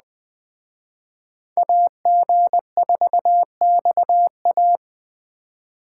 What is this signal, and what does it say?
Morse code, heard as a keyed tone, about 21 wpm, 700 Hz. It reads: AG4XA